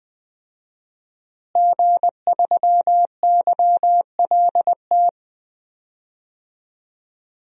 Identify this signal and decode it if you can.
Morse code, heard as a keyed tone, about 20 wpm, 700 Hz. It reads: G3YLT
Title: DXCC-100 Call Signs - 5 Characters - Repeated 2x - 20wpm